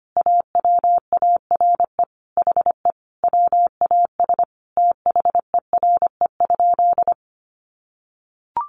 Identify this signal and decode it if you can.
Morse code, heard as a keyed tone, about 25 wpm, 700 Hz. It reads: AWARE HE WAS THERE?
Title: Sentences from Top 1200 Words 25wpm